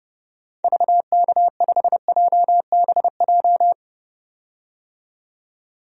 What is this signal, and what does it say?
Morse code, heard as a keyed tone, about 29 wpm, 700 Hz. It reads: VK5JBJ
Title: DXCC-100 Call Signs - 6 Characters - 30wpm